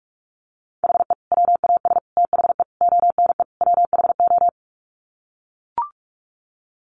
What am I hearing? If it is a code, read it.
Morse code, heard as a keyed tone, about 44 wpm, 700 Hz. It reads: HE WAS THE ONE WHO
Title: Sets of 5 Words Encore2 45wpm